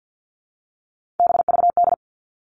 Morse code, heard as a keyed tone, about 50 wpm, 700 Hz. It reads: BVD
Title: Three Challenging Letter-Number Combinations 50wpm